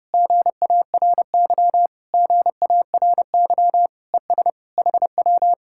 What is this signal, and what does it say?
Morse code, heard as a keyed tone, about 31 wpm, 700 Hz. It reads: GARY GARY ES HW